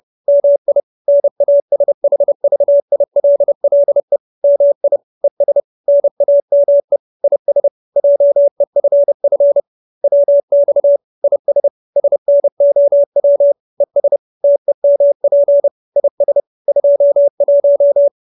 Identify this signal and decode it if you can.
Morse code, heard as a keyed tone, about 30 wpm, 570 Hz. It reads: MI NASHVILLE MI ES NAME IS JEFF WX IS SNOW ES TEMP IS 21